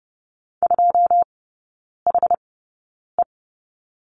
Morse code, heard as a keyed tone, about 30 wpm, spaced as for 9 wpm, 700 Hz. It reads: 2HE
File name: 081 Lesson 14e - Number 2 - Three Characters 30wpm